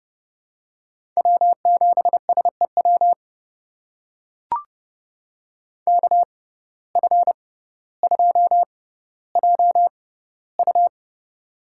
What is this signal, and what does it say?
Morse code, heard as a keyed tone, about 32 wpm, spaced as for 16 wpm, 700 Hz. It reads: W7SEW KF2JU
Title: USA Call Signs - 5 Characters - 6x Character Spacing - 30wpm